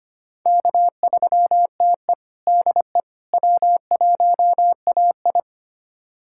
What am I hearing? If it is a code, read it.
Morse code, heard as a keyed tone, about 25 wpm, 700 Hz. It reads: K3TE DE W1AI
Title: USA Callsign DE Callsign - 4 Characters - 25wpm